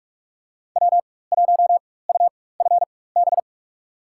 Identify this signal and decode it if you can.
Morse code, heard as a keyed tone, about 46 wpm, spaced as for 21 wpm, 700 Hz. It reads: W1UFB
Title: USA Call Signs - 4x Character Spacing - 45wpm